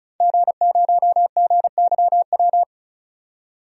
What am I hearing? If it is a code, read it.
Morse code, heard as a keyed tone, about 34 wpm, 700 Hz. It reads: G0GYW
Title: DXCC-100 Call Signs - 5 Characters - 35wpm